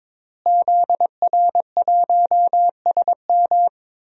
Morse code, heard as a keyed tone, about 22 wpm, 700 Hz. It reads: ZR1SM
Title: DXCC-100 Call Signs - 5 Characters - Repeated 3x - 22wpm